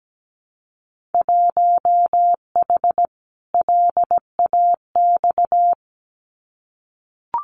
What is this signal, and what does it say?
Morse code, heard as a keyed tone, about 17 wpm, 700 Hz. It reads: 1H LAX
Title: Winter Field Day Exchanges 17wpm